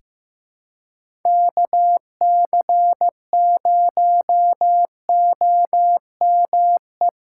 Morse code, heard as a keyed tone, about 15 wpm, 700 Hz. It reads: KC0OME
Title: USA Call Signs - 6 Characters - 3x Character Spacing - 15wpm